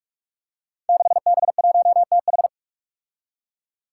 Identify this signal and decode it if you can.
Morse code, heard as a keyed tone, about 44 wpm, 700 Hz. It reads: 6B1TH
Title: Five Challenging Letter-Number Combinations 45wpm